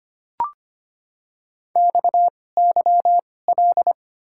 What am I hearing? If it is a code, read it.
Morse code, heard as a keyed tone, about 25 wpm, spaced as for 18 wpm, 700 Hz. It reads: XYL
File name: Top 90 QSO Elements - 2x Character Spacing - 25wpm